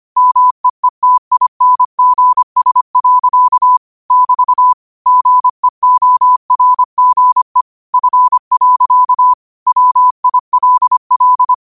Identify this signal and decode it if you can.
Morse code, heard as a keyed tone, about 25 wpm, 1000 Hz. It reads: MEETINGS. <BT> GEORGE F. WILL
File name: qotdfiles / 25wpm